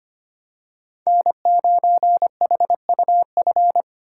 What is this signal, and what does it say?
Morse code, heard as a keyed tone, about 25 wpm, 700 Hz. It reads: N9HUF
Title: USA Call Signs - 8x Character Spacing - 25wpm